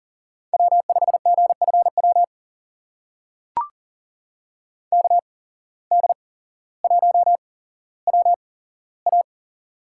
Morse code, heard as a keyed tone, about 40 wpm, spaced as for 17 wpm, 700 Hz. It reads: W5GFW KD1WA
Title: USA Call Signs - 5 Characters - 8x Character Spacing - 40wpm